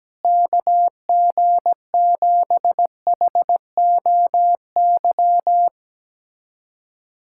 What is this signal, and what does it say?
Morse code, heard as a keyed tone, about 17 wpm, 700 Hz. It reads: KG7HOY
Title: USA Callsign DE Callsign 17wpm